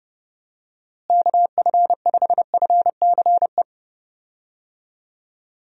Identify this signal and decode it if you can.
Morse code, heard as a keyed tone, about 30 wpm, 700 Hz. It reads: KF5FCE
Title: USA Call Signs - 6 Characters - Repeated 2x - 30wpm